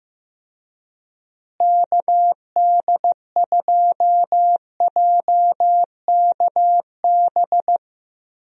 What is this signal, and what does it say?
Morse code, heard as a keyed tone, about 15 wpm, 700 Hz. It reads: KD2JKB
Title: USA Call Signs - 6 Characters - Repeated 2x - 15wpm